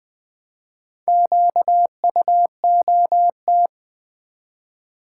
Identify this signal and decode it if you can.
Morse code, heard as a keyed tone, about 20 wpm, 700 Hz. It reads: QUOT
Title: Build A Word - 1 Letter At A Time - 5-Letter Words - 20wpm